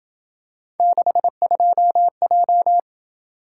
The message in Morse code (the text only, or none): 62J